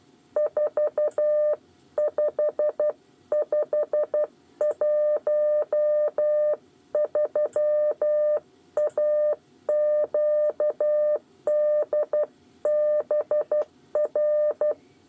Morse code, heard as a keyed tone, about 12 wpm, 600 Hz. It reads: 45513 AQDBR
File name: 8 bis lezione cw 2